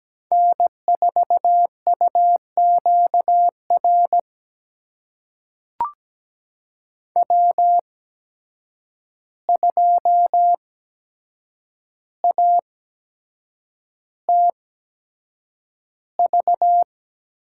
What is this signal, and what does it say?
Morse code, heard as a keyed tone, about 17 wpm, spaced as for 7 wpm, 700 Hz. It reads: N4UQR W2ATV
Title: USA Call Signs - 5 Characters - 8x Character Spacing - 17wpm